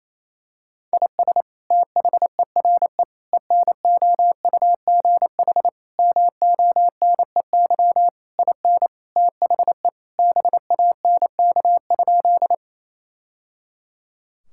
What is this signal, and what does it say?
Morse code, heard as a keyed tone, about 28 wpm, 700 Hz. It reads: IS THERE ENOUGH MONEY IN THE BANK?
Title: Sentences from Top 1000 Words - Review Part 6 - 28wpm